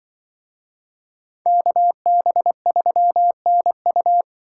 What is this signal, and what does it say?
Morse code, heard as a keyed tone, about 23 wpm, 700 Hz. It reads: KB3NU
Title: Speed Racing - USA 2x2 Call Signs 20wpm